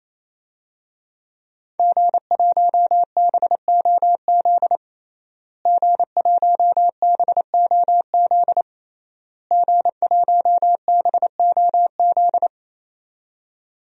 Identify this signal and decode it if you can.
Morse code, heard as a keyed tone, about 28 wpm, 700 Hz. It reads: G1BOZ G1BOZ G1BOZ
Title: DXCC-100 Call Signs - 5 Characters - Repeated 3x - 28wpm